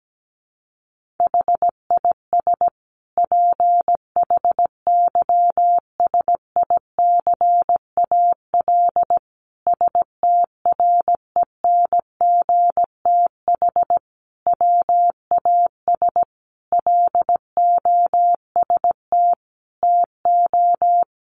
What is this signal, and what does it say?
Morse code, heard as a keyed tone, about 17 wpm, 700 Hz. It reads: HIS PHYSICAL STRENGTH WAS LOST TO
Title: Sentences from Top 1400 Words 17wpm